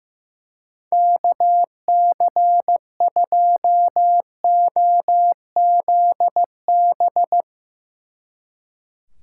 Morse code, heard as a keyed tone, about 15 wpm, 700 Hz. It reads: KC2OZB
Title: Speed Racing - USA 2x3 Call Signs 15wpm